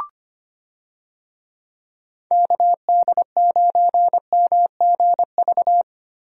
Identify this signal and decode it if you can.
Morse code, heard as a keyed tone, about 25 wpm, 700 Hz. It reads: KD9MGV
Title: USA Call Signs - 6 Characters - Repeated 2x - 25wpm